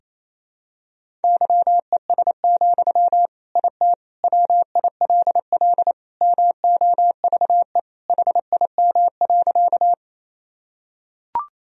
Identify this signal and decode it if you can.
Morse code, heard as a keyed tone, about 28 wpm, 700 Hz. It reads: YES, IT WILL MOVE HIM.
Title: Sentences from Top 1000 Words - Review Part 1 - 28wpm